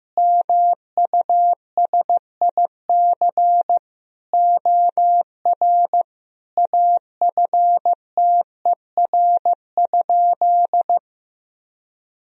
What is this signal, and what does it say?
Morse code, heard as a keyed tone, about 15 wpm, 700 Hz. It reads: MUSIC OR AFTER?
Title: Sentences from Top 1000 Words - Review Part 1 - 15wpm